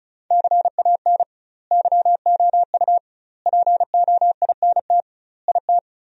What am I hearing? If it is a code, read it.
Morse code, heard as a keyed tone, about 34 wpm, 700 Hz. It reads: CAN YOU POINT IT
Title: 2x Word-Spacing - Sentences from Top 200 Words - 35wpm